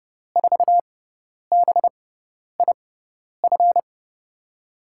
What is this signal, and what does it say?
Morse code, heard as a keyed tone, about 30 wpm, spaced as for 10 wpm, 700 Hz. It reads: 4BIF
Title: USA Call Signs - 6x Character Spacing - 30wpm